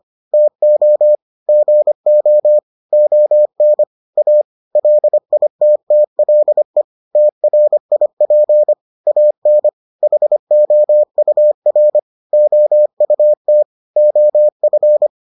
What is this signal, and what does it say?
Morse code, heard as a keyed tone, about 25 wpm, 600 Hz. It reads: TO GO ON A LITTLE TRIP AN HOUR OUT OF